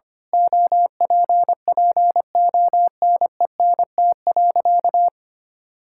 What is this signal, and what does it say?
Morse code, heard as a keyed tone, about 25 wpm, 700 Hz. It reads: OPPONENT.
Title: Pangram Challenge v2 - 25wpm